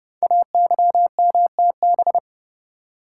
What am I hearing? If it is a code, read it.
Morse code, heard as a keyed tone, about 29 wpm, 700 Hz. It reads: AYMTB
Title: Code Groups - 5 Letters - 30wpm